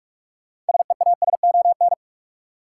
Morse code, heard as a keyed tone, about 45 wpm, 700 Hz. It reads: SEASON
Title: Sentences from Top 1400 Words - New Word Drill Alt - 45wpm